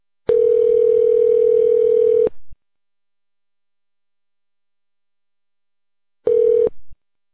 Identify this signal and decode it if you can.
Ringback tone (North America), 440 + 480 Hz.